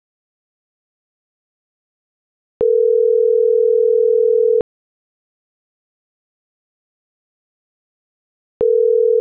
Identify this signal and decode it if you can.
Ringback tone (North America), 440 + 480 Hz.